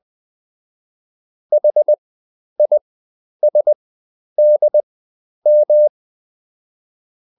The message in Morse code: HISDM